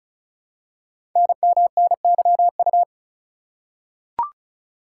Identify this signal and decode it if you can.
Morse code, heard as a keyed tone, about 36 wpm, 700 Hz. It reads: NMNYU
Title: Code Groups - 5 Letters - 35wpm